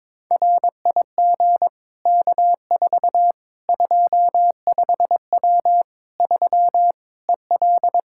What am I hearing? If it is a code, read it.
Morse code, heard as a keyed tone, about 22 wpm, 700 Hz. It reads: RIG K4 25W 3 EL
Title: Practice QSO - Exchange 2 - Info - 22wpm